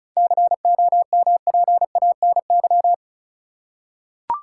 Morse code, heard as a keyed tone, about 34 wpm, 700 Hz. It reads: COMPANY
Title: Seven Letter Words - 35wpm